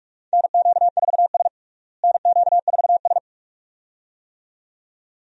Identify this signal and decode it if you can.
Morse code, heard as a keyed tone, about 46 wpm, 700 Hz. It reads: NQ4S NQ4S
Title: USA Call Signs - 4 Characters - Repeated 2x - 45wpm